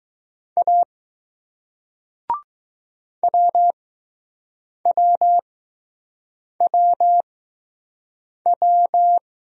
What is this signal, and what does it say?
Morse code, heard as a keyed tone, about 20 wpm, spaced as for 6 wpm, 700 Hz. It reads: A WWWW